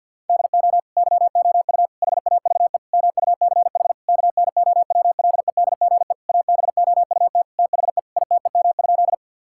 Morse code, heard as a keyed tone, about 52 wpm, 700 Hz. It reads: DO YOU HAVE MUCH KNOWLEDGE ABOUT THE ITEM?
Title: Sentences from Top 1000 Words - Review Part 6 - 50wpm